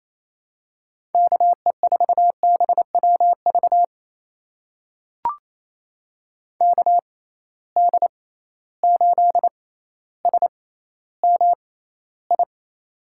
Morse code, heard as a keyed tone, about 28 wpm, spaced as for 14 wpm, 700 Hz. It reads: KE4BWV KD8SMI
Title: USA Call Signs - 6 Characters - 6x Character Spacing - 28wpm